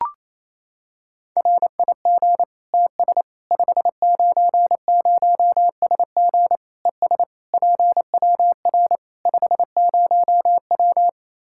Morse code, heard as a keyed tone, about 28 wpm, 700 Hz. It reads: RIG TS 590SG ES PWR 50W
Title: K1USN SST QSO Practice 28wpm